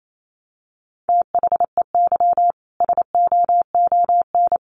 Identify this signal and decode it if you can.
Morse code, heard as a keyed tone, about 28 wpm, 700 Hz. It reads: THEY SOON
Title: Sentences from Top 1200 Words 28wpm